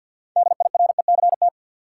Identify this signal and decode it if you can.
Morse code, heard as a keyed tone, about 53 wpm, 700 Hz. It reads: DIRECT
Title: Sentences from Top 400 Words - New Word Drill - 50wpm